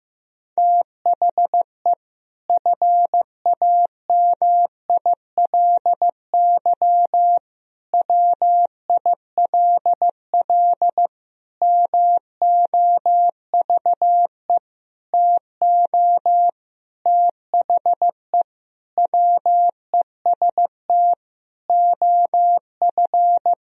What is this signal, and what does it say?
Morse code, heard as a keyed tone, about 16 wpm, 700 Hz. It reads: THE FAMILY WILL MOVE TO THE WEST OF
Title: Sentences from Top 1000 Words - Review Part 2 - 15wpm